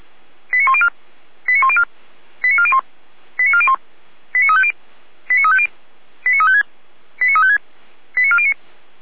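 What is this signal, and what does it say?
Five-tone selective call (ZVEI), heard as quick runs of five tones, tones of about 70 ms. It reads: 89104 89104 89401 89401 89350 89350 89356 89356 89409
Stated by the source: ZVEI five-tone call received by SDR, recorded at feuerwehr-orlamuende.org fire brigade.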